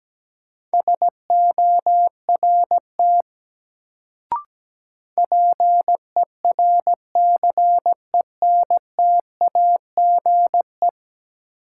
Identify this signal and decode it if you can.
Morse code, heard as a keyed tone, about 17 wpm, 700 Hz. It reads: SORT PERCENTAGE
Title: Sentences from Top 1300 Words - New Word Drill Alt - 17wpm